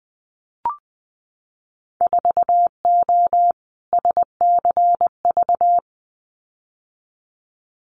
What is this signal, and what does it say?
Morse code, heard as a keyed tone, about 20 wpm, 700 Hz. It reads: 4O SCV
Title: Winter Field Day Exchanges 20wpm